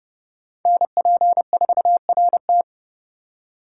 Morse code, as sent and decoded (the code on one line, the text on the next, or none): -. .--. ....- .-. -
NP4RT